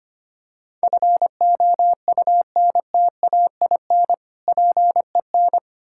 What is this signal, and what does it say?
Morse code, heard as a keyed tone, about 25 wpm, 700 Hz. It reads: FOUNTAIN PEN
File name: practice / 25wpm / Difficult Words with Hints - 25wpm